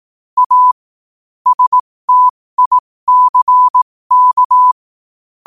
Morse code, heard as a keyed tone, about 18 wpm, spaced as for 15 wpm, 1000 Hz. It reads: A STICK
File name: qotdfiles / 15wpm